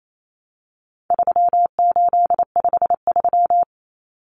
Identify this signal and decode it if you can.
Morse code, heard as a keyed tone, about 28 wpm, 700 Hz. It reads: 3853
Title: Common Cut Numbers - 4-Digit 28wpm